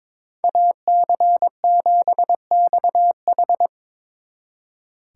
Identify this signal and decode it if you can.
Morse code, heard as a keyed tone, about 26 wpm, 700 Hz. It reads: AC7XH
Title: Speed Racing - USA 2x2 Call Signs 22wpm